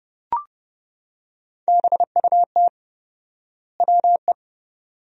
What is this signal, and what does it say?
Morse code, heard as a keyed tone, about 30 wpm, 700 Hz. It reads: BUT WE
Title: 4x Word-Spacing - Sentences from Top 100 Words - 30wpm